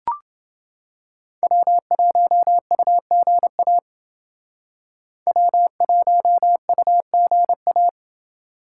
Morse code, heard as a keyed tone, about 28 wpm, 700 Hz. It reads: W1UGA W1UGA